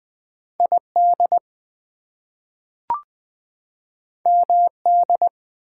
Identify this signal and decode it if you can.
Morse code, heard as a keyed tone, about 20 wpm, 700 Hz. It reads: ID MD
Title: US States and Canadian Provinces 20wpm